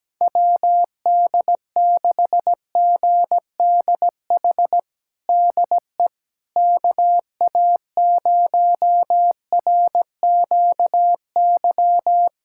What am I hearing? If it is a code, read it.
Morse code, heard as a keyed tone, about 17 wpm, 700 Hz. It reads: WD6GDH DE KA0RQY
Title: USA Callsign DE Callsign - 6 Characters - 17wpm